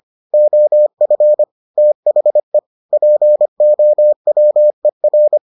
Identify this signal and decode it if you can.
Morse code, heard as a keyed tone, about 25 wpm, 600 Hz. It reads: OF THE POWER